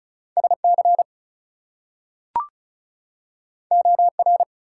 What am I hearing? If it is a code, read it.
Morse code, heard as a keyed tone, about 36 wpm, 700 Hz. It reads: SC OR
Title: US States and Canadian Provinces 35wpm